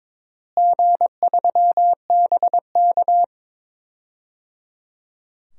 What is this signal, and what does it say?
Morse code, heard as a keyed tone, about 22 wpm, 700 Hz. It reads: G3BK
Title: DXCC-100 Call Signs - 4 Characters - Repeated 3x - 22wpm